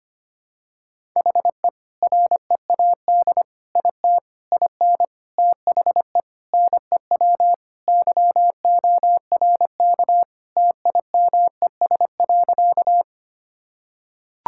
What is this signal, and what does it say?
Morse code, heard as a keyed tone, about 25 wpm, 700 Hz. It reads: HE READ IT IN THE NEW YORK TIMES.
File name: practice / 25wpm / Sentences with 4-Word Phrases - 25wpm